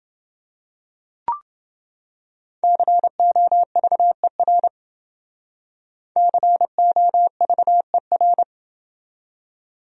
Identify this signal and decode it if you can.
Morse code, heard as a keyed tone, about 28 wpm, 700 Hz. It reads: COVER COVER